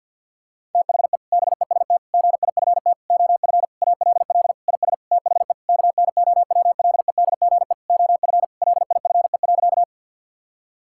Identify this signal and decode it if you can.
Morse code, heard as a keyed tone, about 52 wpm, 700 Hz. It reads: THE BEST GIFT OF ALL IS THE KNOWLEDGE OF LIFE.
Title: Sentences from Top 1000 Words - Review Part 6 - 50wpm